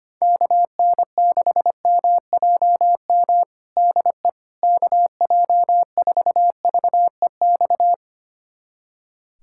Morse code, heard as a keyed tone, about 25 wpm, 700 Hz. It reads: KN6MJM DE KJ4VEX
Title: USA Callsign DE Callsign - 6 Characters - 25wpm